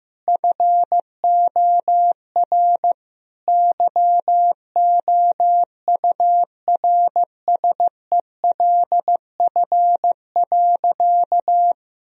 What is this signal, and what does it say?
Morse code, heard as a keyed tone, about 16 wpm, 700 Hz. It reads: FOR YOURSELF.
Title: Sentences from Top 1000 Words - Review Part 4 - 15wpm